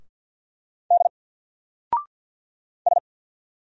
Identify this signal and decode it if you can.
Morse code, heard as a keyed tone, about 48 wpm, 700 Hz. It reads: D S
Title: Single Character (Letter:Number:Punctuation:Prosign) 50wpm